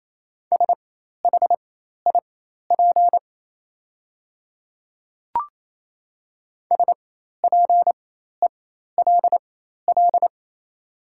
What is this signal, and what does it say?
Morse code, heard as a keyed tone, about 28 wpm, spaced as for 13 wpm, 700 Hz. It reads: SHIP SPELL
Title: 4x Character Spacing - Top 500 Words - 28wpm